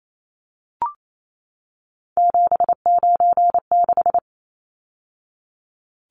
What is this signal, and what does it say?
Morse code, heard as a keyed tone, about 28 wpm, 700 Hz. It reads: 796
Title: Numbers - 3-Digit 28wpm